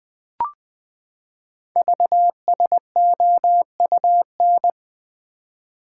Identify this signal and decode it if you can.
Morse code, heard as a keyed tone, about 20 wpm, 700 Hz. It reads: VSOUN
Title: Code Groups - 5 Letters - 20wpm